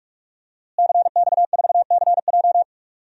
Morse code, heard as a keyed tone, about 46 wpm, 700 Hz. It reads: CX4CJ
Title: DXCC-100 Call Signs - 5 Characters - Repeated 3x - 45wpm